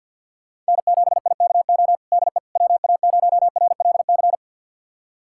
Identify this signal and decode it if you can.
Morse code, heard as a keyed tone, about 47 wpm, 700 Hz. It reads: N7IKO DE WA0RLC
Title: USA Callsign DE Callsign 50wpm